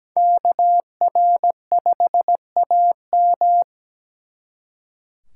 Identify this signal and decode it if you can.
Morse code, heard as a keyed tone, about 20 wpm, 700 Hz. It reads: KR5AM
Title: Speed Racing - USA 2x2 Call Signs 17wpm